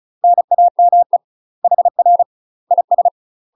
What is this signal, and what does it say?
Morse code, heard as a keyed tone, about 35 wpm, 700 Hz. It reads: NAME HR IS
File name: UK MASTER QSOs TRAINING / 35WPM / UK MASTER QSOs 0026